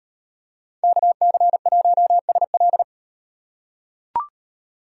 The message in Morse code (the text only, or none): KC1SL